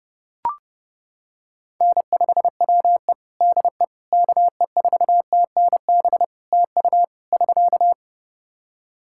Morse code, heard as a keyed tone, about 30 wpm, 700 Hz. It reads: N5WE DE KE4TNB TU <SK>
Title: K1USN SST QSO Practice 30wpm